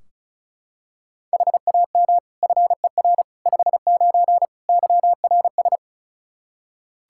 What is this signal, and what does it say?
Morse code, heard as a keyed tone, about 36 wpm, 700 Hz. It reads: HAM FER 59 YRS
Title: Practice QSO - Exchange 3 - More Info - 35wpm